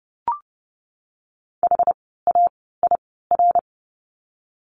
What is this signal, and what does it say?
Morse code, heard as a keyed tone, about 30 wpm, spaced as for 17 wpm, 700 Hz. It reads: HAIR
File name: Lesson 17f - Letter u - Words 30wpm